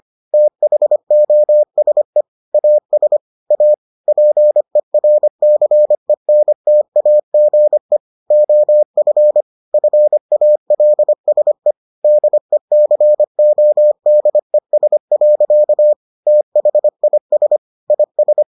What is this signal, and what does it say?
Morse code, heard as a keyed tone, about 25 wpm, 600 Hz. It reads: THOSE AS A PERCENTAGE OF FALSE DECODES. THIS IS